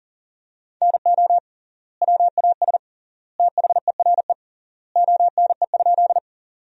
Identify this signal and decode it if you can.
Morse code, heard as a keyed tone, about 40 wpm, 700 Hz. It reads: NO WAS THERE ONE?
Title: 3x Word-Spacing - Sentences from Top 100 Words - 40wpm